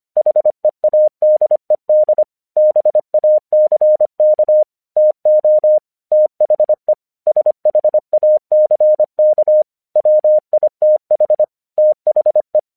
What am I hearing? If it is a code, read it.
Morse code, heard as a keyed tone, about 25 wpm, 610 Hz. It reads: HEADED BACK TO THE SHACK WITH THE
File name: 1 English 25wpm / AQC22